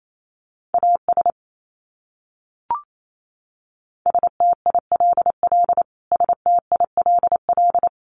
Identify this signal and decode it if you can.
Morse code, heard as a keyed tone, about 28 wpm, 700 Hz. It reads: AS STILL STILL
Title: Top 300 Words Repeated 28wpm